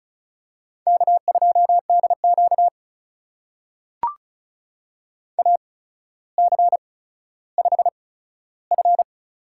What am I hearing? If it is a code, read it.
Morse code, heard as a keyed tone, about 36 wpm, spaced as for 15 wpm, 700 Hz. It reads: K2DQ AC5F
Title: USA Call Signs - 4 Characters - 8x Character Spacing - 35wpm